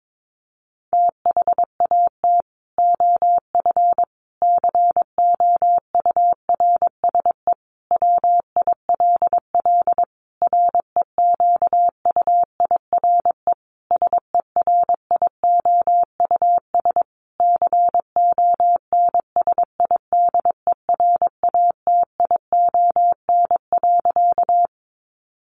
Morse code, heard as a keyed tone, about 22 wpm, 700 Hz. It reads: THAT OF COURSE WILL REQUIRE SERIOUS CONSIDERATION.
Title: Sentences from Top 1500 Words 22wpm